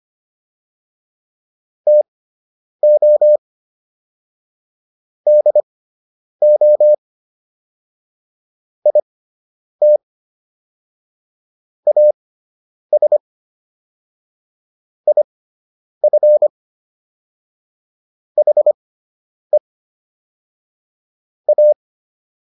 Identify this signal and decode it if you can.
Morse code, heard as a keyed tone, about 25 wpm, spaced as for 9 wpm, 600 Hz. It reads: TO DO IT AS IF HE A